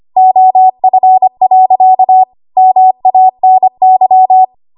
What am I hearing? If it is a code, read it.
Morse code, heard as a keyed tone, about 25 wpm, 750 Hz. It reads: OF. MANY
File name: Archive / 25 WPM / 25WPM